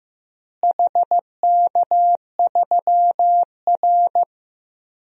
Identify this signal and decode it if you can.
Morse code, heard as a keyed tone, about 16 wpm, 700 Hz. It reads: HK3R
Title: DXCC-100 Call Signs - 4 Characters - Repeated 3x - 15wpm